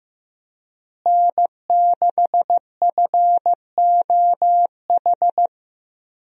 Morse code, heard as a keyed tone, about 15 wpm, 700 Hz. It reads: N6FOH
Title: USA Call Signs - 5 Characters - Repeated 3x - 15wpm